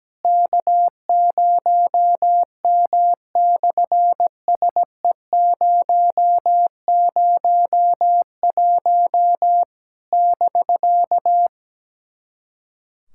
Morse code, heard as a keyed tone, about 17 wpm, 700 Hz. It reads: K0M/SE001 <BK>
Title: SOTA QSO Practice 17wpm